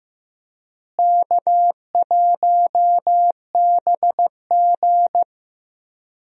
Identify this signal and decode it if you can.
Morse code, heard as a keyed tone, about 16 wpm, 700 Hz. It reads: K1BG
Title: USA Call Signs - 4 Characters - Repeated 2x - 15wpm